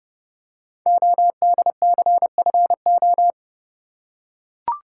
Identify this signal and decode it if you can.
Morse code, heard as a keyed tone, about 30 wpm, 690 Hz. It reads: ODCFO
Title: Code Groups - 5 Letters - 30wpm